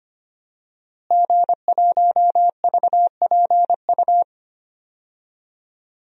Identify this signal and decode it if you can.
Morse code, heard as a keyed tone, about 25 wpm, 700 Hz. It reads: G1VPU
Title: DXCC-100 Call Signs - 5 Characters - Repeated 3x - 25wpm